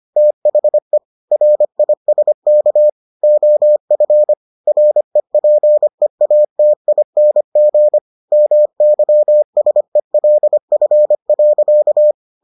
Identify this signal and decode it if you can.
Morse code, heard as a keyed tone, about 25 wpm, 600 Hz. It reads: THE RISK OF REPEATING MYSELF.